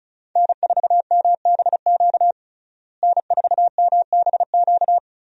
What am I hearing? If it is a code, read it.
Morse code, heard as a keyed tone, about 36 wpm, 700 Hz. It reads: N4MBQ N4MBQ
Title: USA Call Signs - 5 Characters - Repeated 3x - 35wpm